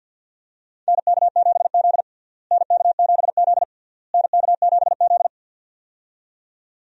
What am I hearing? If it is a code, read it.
Morse code, heard as a keyed tone, about 52 wpm, 700 Hz. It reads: NK7Z NK7Z NK7Z
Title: USA Call Signs - 4 Characters - Repeated 3x - 50wpm